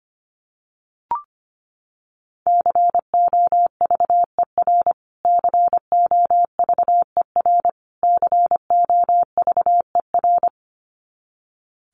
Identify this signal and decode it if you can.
Morse code, heard as a keyed tone, about 25 wpm, 700 Hz. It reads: COVER COVER COVER
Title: Top 500 Words Repeated 25wpm